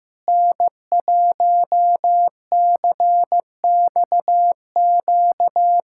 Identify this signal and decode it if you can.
Morse code, heard as a keyed tone, about 16 wpm, 700 Hz. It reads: N1CXQ
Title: USA Callsign DE Callsign - 5 Characters - 15wpm